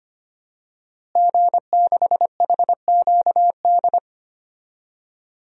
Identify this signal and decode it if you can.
Morse code, heard as a keyed tone, about 25 wpm, 700 Hz. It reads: G6HQD
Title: DXCC-100 Call Signs - 5 Characters - Repeated 3x - 25wpm